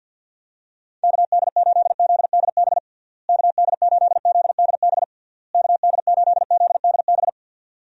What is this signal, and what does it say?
Morse code, heard as a keyed tone, about 52 wpm, 700 Hz. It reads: KD8ZDB KD8ZDB KD8ZDB
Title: USA Call Signs - 6 Characters - Repeated 3x - 50wpm